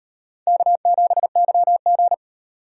Morse code, heard as a keyed tone, about 38 wpm, 700 Hz. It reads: K7YG